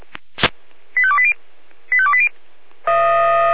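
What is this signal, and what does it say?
Five-tone selective call (ZVEI), heard as quick runs of five tones, tones of about 70 ms. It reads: 85280 85280
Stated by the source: ZVEI five-tone selective call received by SDR.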